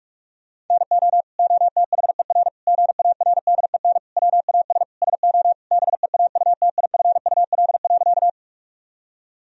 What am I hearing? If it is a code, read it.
Morse code, heard as a keyed tone, about 41 wpm, 700 Hz. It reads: NO OTHER GARDEN WAS SO BEAUTIFUL.
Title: Sentences from Top 1000 Words - Review Part 4 - 45wpm